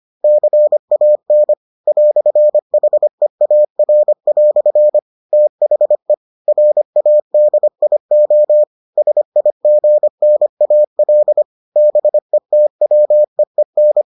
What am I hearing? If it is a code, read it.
Morse code, heard as a keyed tone, about 25 wpm, 600 Hz. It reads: CAN "HEAR" THE RADIO SIGNAL BETWEEN